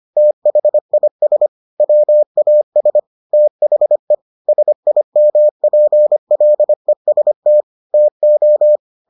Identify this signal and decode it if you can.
Morse code, heard as a keyed tone, about 25 wpm, 600 Hz. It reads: THIS WAS THE SIMPLEST TO